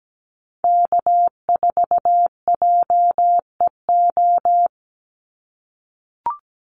Morse code, heard as a keyed tone, about 17 wpm, 700 Hz. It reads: K4JEO
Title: USA Call Signs 17wpm